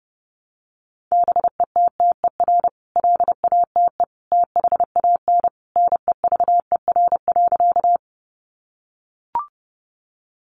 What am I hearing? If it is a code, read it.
Morse code, heard as a keyed tone, about 30 wpm, 700 Hz. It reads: BETTER LATE THAN NEVER.
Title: Sentences from Top 400 Words 30wpm